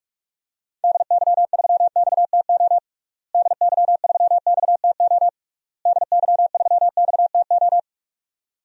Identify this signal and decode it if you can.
Morse code, heard as a keyed tone, about 42 wpm, 700 Hz. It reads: DY3XTO DY3XTO DY3XTO
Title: DXCC-100 Call Signs - 6 Characters - Repeated 3x - 45wpm